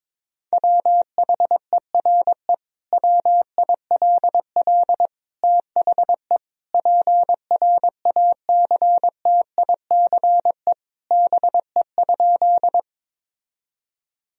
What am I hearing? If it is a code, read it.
Morse code, heard as a keyed tone, about 22 wpm, 700 Hz. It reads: WHERE WILL THE PRACTICE BE?
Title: Sentences from Top 1000 Words - Review Part 4 - 22wpm